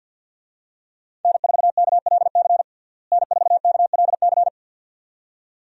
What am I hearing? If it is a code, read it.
Morse code, heard as a keyed tone, about 51 wpm, 700 Hz. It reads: N4KLC N4KLC
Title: USA Call Signs - 5 Characters - Repeated 2x - 50wpm